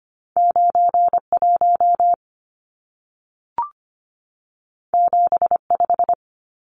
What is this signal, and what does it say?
Morse code, heard as a keyed tone, about 25 wpm, 700 Hz. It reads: 91 75
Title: Numbers - 2-Digit 25wpm